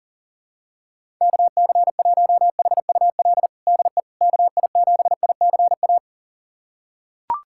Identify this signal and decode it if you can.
Morse code, heard as a keyed tone, about 40 wpm, 700 Hz. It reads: KC1HUL DE KI7ICA
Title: USA Callsign DE Callsign - 6 Characters - 40wpm